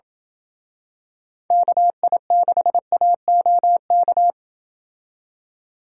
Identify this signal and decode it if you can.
Morse code, heard as a keyed tone, about 27 wpm, 700 Hz. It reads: KI6AOK